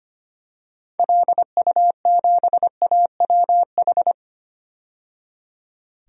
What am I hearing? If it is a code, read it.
Morse code, heard as a keyed tone, about 25 wpm, 700 Hz. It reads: LU7AWH
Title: DXCC-100 Call Signs - 6 Characters - 25wpm